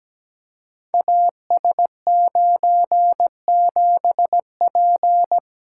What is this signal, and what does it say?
Morse code, heard as a keyed tone, about 17 wpm, 700 Hz. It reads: AS97P
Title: Code Groups - 5 Letter:Numbers - 17wpm